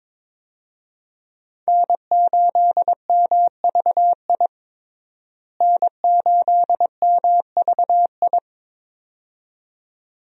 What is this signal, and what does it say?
Morse code, heard as a keyed tone, about 22 wpm, 700 Hz. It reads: N8MVI N8MVI
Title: USA Call Signs - 5 Characters - Repeated 2x - 22wpm